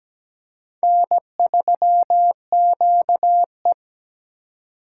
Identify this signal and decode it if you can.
Morse code, heard as a keyed tone, about 17 wpm, 700 Hz. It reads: N3QE
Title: K1USN SST QSO Practice - 17wpm